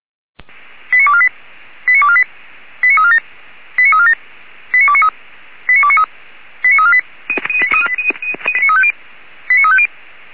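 Five-tone selective call (ZVEI), heard as quick runs of five tones, tones of about 70 ms. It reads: 89248 89248 89347 89347 89293 89293 89348 89350
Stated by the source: ZVEI five-tone selective call received on an SDR.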